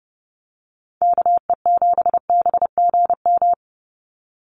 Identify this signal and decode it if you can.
Morse code, heard as a keyed tone, about 30 wpm, 700 Hz. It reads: KE7BGM
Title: NA SOTA Activator Callsigns 30wpm